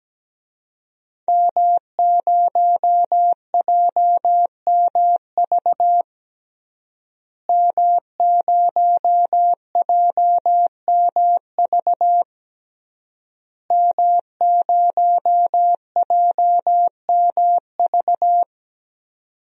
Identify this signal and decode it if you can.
Morse code, heard as a keyed tone, about 17 wpm, 700 Hz. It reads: M0JMV M0JMV M0JMV
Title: DXCC-100 Call Signs - 5 Characters - Repeated 3x - 17wpm